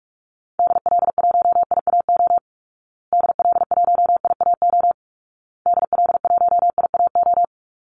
Morse code, heard as a keyed tone, about 44 wpm, 700 Hz. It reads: DL1IAO DL1IAO DL1IAO
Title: Call Signs Repeated 2x - 45wpm